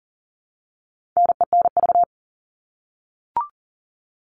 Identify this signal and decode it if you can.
Morse code, heard as a keyed tone, about 40 wpm, 700 Hz. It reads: NENV
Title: All Cut Numbers - 4-Digit 40wpm